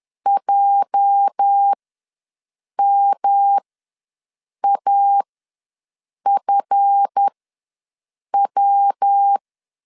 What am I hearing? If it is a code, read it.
Morse code, heard as a keyed tone, about 10 wpm, spaced as for 6 wpm, 800 Hz. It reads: JMAFW
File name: LEZIONI / 8 YJ45 / 8 YJ4 ws 10wpm 4020